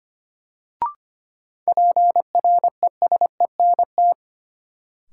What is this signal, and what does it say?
Morse code, heard as a keyed tone, about 25 wpm, 700 Hz. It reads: PRESENT
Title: Sentences from Top 800 Words - New Word Drill - 25wpm